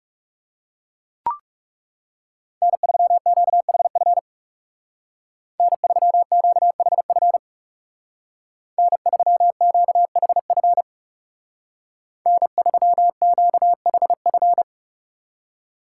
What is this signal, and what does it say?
Morse code, heard as a keyed tone, about 36 wpm, 700 Hz. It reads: N3QHF N3QHF N3QHF N3QHF
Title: Speed Racing - USA 1x3 Call Signs 30wpm